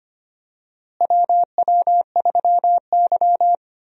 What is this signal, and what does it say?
Morse code, heard as a keyed tone, about 25 wpm, 700 Hz. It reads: WW3Y
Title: USA Call Signs - 4 Characters - Repeated 3x - 25wpm